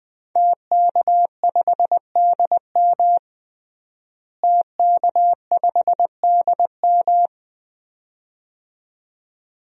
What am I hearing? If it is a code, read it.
Morse code, heard as a keyed tone, about 20 wpm, 700 Hz. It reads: TK5DM TK5DM
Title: DXCC-100 Call Signs - 5 Characters - Repeated 2x - 20wpm